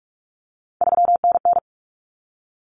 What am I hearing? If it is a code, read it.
Morse code, heard as a keyed tone, about 44 wpm, 700 Hz. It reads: 3NN
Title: Common Cut Numbers - 3-Digit 45wpm